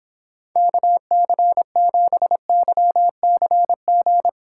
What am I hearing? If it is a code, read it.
Morse code, heard as a keyed tone, about 26 wpm, 700 Hz. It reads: KC7YCG